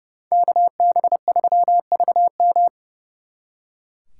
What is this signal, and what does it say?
Morse code, heard as a keyed tone, about 36 wpm, 700 Hz. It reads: KB3VM
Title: Speed Racing - USA 2x2 Call Signs 30wpm